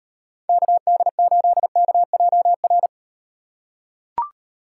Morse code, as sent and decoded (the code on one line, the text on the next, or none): -.- -.. ---.. -.- .--- .-.
KD8KJR